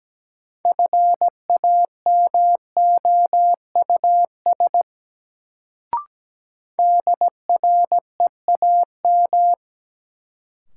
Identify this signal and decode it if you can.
Morse code, heard as a keyed tone, about 17 wpm, 700 Hz. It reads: FAMOUS DREAM
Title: Sentences from Top 1500 Words - New Word Drill - 17wpm